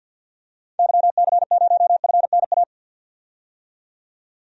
Morse code, heard as a keyed tone, about 51 wpm, 700 Hz. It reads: YC0FNA
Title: DXCC-100 Call Signs - 6 Characters - 50wpm